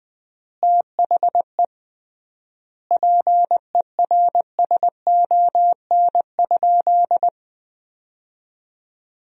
Morse code, heard as a keyed tone, about 21 wpm, 700 Hz. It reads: THE PERSON?
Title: 3x Word-Spacing - Sentences from Top 400 Words - 20wpm